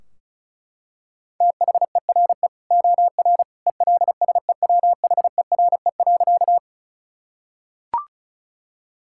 Morse code, heard as a keyed tone, about 36 wpm, 700 Hz. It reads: THERE OR ELSEWHERE.
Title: Sentences from Top 1600 Words 35wpm